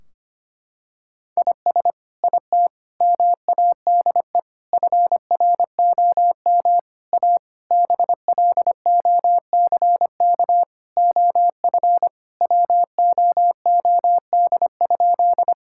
Morse code, heard as a keyed tone, about 25 wpm, 700 Hz. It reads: IS IT MADE FROM A BLOCK OF WOOD?
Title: Sentences from Top 1600 Words 25wpm